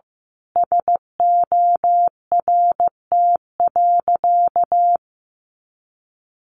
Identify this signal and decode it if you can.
Morse code, heard as a keyed tone, about 15 wpm, 700 Hz. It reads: SORT.
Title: Sentences from Top 1500 Words 15wpm